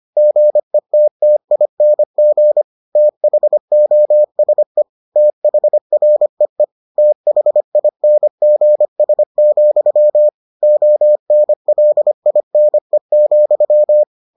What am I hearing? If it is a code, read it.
Morse code, heard as a keyed tone, about 25 wpm, 600 Hz. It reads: GETTING THOSE THREE THINGS, ONLINE,